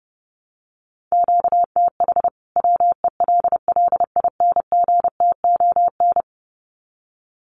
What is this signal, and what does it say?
Morse code, heard as a keyed tone, about 30 wpm, 700 Hz. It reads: QTH WELLINGTON
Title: Common QSO Phrases 30wpm